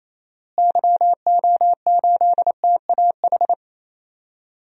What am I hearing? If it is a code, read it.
Morse code, heard as a keyed tone, about 28 wpm, 700 Hz. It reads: YO8TAH
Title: DXCC-100 Call Signs - 6 Characters - Repeated 2x - 28wpm